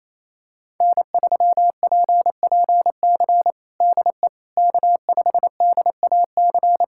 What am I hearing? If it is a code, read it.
Morse code, heard as a keyed tone, about 28 wpm, 700 Hz. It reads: N3PPC DE K5DAC
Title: K1USN SST QSO Practice 28wpm